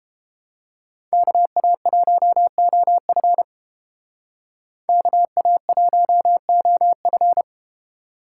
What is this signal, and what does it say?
Morse code, heard as a keyed tone, about 30 wpm, 700 Hz. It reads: KA1OF KA1OF